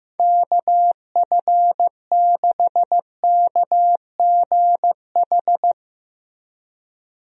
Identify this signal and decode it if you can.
Morse code, heard as a keyed tone, about 16 wpm, 700 Hz. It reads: KF6KGH
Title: USA Call Signs - 6 Characters - Repeated 3x - 15wpm